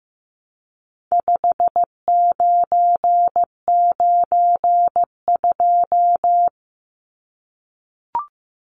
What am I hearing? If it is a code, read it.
Morse code, heard as a keyed tone, about 15 wpm, 700 Hz. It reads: 5992
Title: Numbers - 4-Digit 15wpm